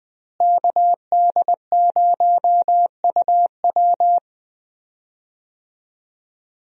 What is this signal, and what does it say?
Morse code, heard as a keyed tone, about 20 wpm, 700 Hz. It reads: KD0UW
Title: USA Call Signs - 5 Characters - Repeated 2x - 20wpm